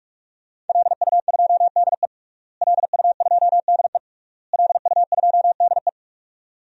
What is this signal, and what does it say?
Morse code, heard as a keyed tone, about 41 wpm, 700 Hz. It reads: LU2DE LU2DE LU2DE
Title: DXCC-100 Call Signs - 5 Characters - Repeated 3x - 45wpm